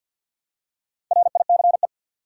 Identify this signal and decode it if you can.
Morse code, heard as a keyed tone, about 46 wpm, 700 Hz. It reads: RICE
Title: Four Letter Words - 50wpm